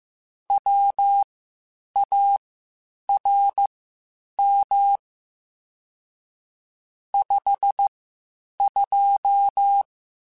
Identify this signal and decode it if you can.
Morse code, heard as a keyed tone, about 15 wpm, spaced as for 8 wpm, 800 Hz. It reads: WARM 52
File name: Spaced Characters / Spaced Characters-mp3s / Weather 4 10 sp @13